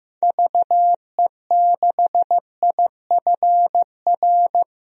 Morse code, heard as a keyed tone, about 15 wpm, 700 Hz. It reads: VE6IFR
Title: DXCC-100 Call Signs - 6 Characters - Repeated 3x - 15wpm